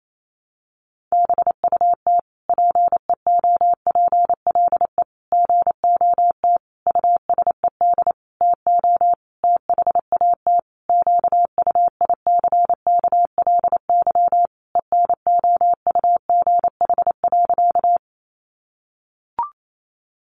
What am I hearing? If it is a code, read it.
Morse code, heard as a keyed tone, about 28 wpm, 700 Hz. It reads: BUT PEOPLE GOT USED TO THAT QUICKLY ENOUGH.
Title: Sentences from Top 700 Words 28wpm